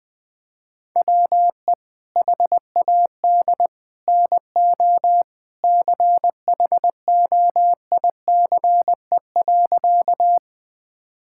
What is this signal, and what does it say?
Morse code, heard as a keyed tone, about 21 wpm, 700 Hz. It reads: WE HAD NO CHOICE.
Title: Sentences from Top 1000 Words - Review Part 6 - 20wpm